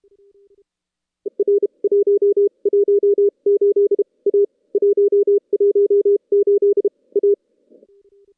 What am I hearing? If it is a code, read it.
Morse code, heard as a keyed tone, about 36 wpm, spaced as for 29 wpm, 400 Hz. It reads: R 118 A 118 A
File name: contests / CW-SS-k5tr- -2300